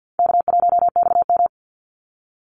Morse code, heard as a keyed tone, about 50 wpm, 700 Hz. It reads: K1XM
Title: Call Signs Repeated 2x - 50wpm